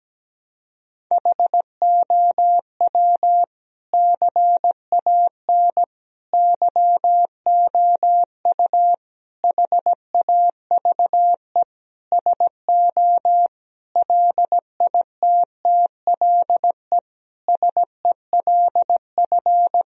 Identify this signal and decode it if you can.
Morse code, heard as a keyed tone, about 17 wpm, 700 Hz. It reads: HOW CAN YOU HAVE SO LITTLE SELF
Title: Sentences from Top 1000 Words - Review Part 1 - 17wpm